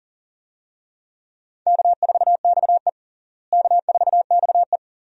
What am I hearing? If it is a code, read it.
Morse code, heard as a keyed tone, about 40 wpm, 700 Hz. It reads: K4XE K4XE
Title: USA Call Signs - 4 Characters - Repeated 2x - 40wpm